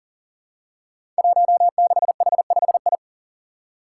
Morse code, heard as a keyed tone, about 38 wpm, 700 Hz. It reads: 16H5I
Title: Five Challenging Letter-Number Combinations 40wpm